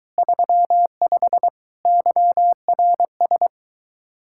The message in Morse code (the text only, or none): 35 YRS